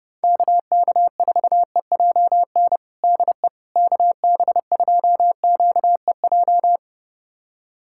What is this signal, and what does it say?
Morse code, heard as a keyed tone, about 30 wpm, 700 Hz. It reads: KK4EJN DE KB2QEJ
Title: USA Callsign DE Callsign 30wpm